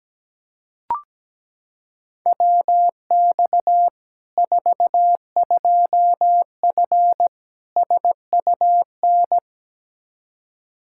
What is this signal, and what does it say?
Morse code, heard as a keyed tone, about 17 wpm, 700 Hz. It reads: WX 42F SUN
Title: K1USN SST QSO Practice 17wpm